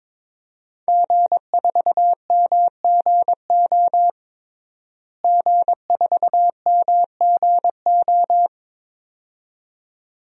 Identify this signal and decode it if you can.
Morse code, heard as a keyed tone, about 22 wpm, 700 Hz. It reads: G4MGO G4MGO
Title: DXCC-100 Call Signs - 5 Characters - Repeated 2x - 22wpm